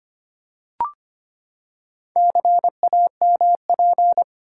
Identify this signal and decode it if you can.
Morse code, heard as a keyed tone, about 25 wpm, 700 Hz. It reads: CAMP
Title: Four Letter Words - 25wpm